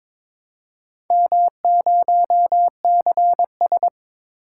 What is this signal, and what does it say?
Morse code, heard as a keyed tone, about 22 wpm, 700 Hz. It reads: M0CS
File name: practice / 22wpm / Four Letter-Number Combinations - 22wpm